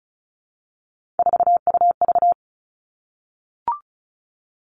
Morse code, heard as a keyed tone, about 36 wpm, 700 Hz. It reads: 4UV
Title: Three Challenging Letter-Number Combinations 35wpm